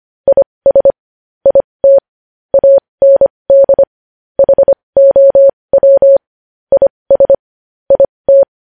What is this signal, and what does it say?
Morse code, heard as a keyed tone, about 25 wpm, spaced as for 20 wpm, 550 Hz. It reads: IS IT AND HOW IS IT
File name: podcast / News headlines-1x-25x20